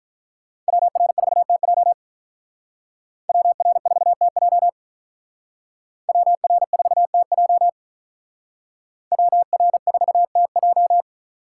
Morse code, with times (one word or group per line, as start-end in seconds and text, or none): 0.7-1.9 WR4TJ
3.3-4.7 WR4TJ
6.1-7.7 WR4TJ
9.1-11.0 WR4TJ